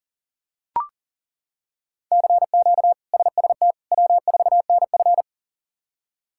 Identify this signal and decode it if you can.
Morse code, heard as a keyed tone, about 40 wpm, 700 Hz. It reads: CQ SST W4NF
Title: K1USN SST QSO Practice - 40wpm